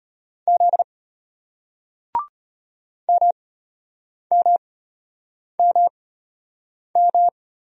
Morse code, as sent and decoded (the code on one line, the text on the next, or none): --.. / -- -- -- --
Z MMMM